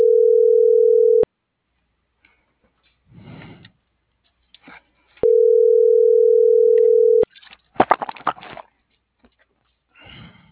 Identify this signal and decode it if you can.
Ringback tone (North America), 440 + 480 Hz.